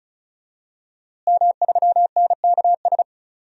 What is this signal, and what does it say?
Morse code, heard as a keyed tone, about 34 wpm, 700 Hz. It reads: M3NKS
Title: DXCC-100 Call Signs - 5 Characters - Repeated 3x - 35wpm